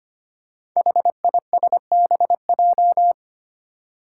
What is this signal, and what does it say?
Morse code, heard as a keyed tone, about 25 wpm, 700 Hz. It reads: HISBJ
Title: Five Challenging Letter-Number Combinations 25wpm